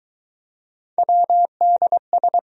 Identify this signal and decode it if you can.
Morse code, heard as a keyed tone, about 20 wpm, 700 Hz. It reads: WDS